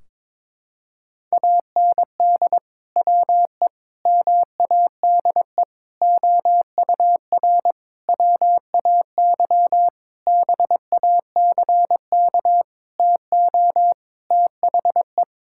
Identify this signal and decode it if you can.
Morse code, heard as a keyed tone, about 22 wpm, 700 Hz. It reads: AND WE MADE OUR WAY BACK TO THE
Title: Sentences from Top 1600 Words 22wpm